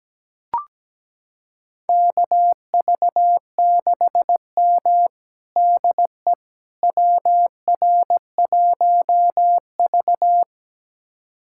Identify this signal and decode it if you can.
Morse code, heard as a keyed tone, about 17 wpm, 700 Hz. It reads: KV6M DE WR1V
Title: USA Callsign DE Callsign - 4 Characters - 17wpm